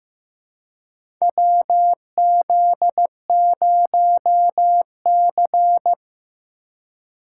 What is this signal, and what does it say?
Morse code, heard as a keyed tone, about 15 wpm, 700 Hz. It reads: WZ0C
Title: USA Call Signs - 4 Characters - 2x Character Spacing - 15wpm